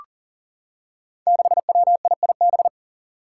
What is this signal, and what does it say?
Morse code, heard as a keyed tone, about 40 wpm, 700 Hz. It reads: 6WIIB
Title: Five Challenging Letter-Number Combinations 40wpm